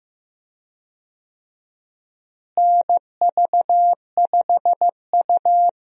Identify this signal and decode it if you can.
Morse code, heard as a keyed tone, about 16 wpm, 700 Hz. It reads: NV5U
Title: USA Call Signs - 4 Characters - Repeated 3x - 15wpm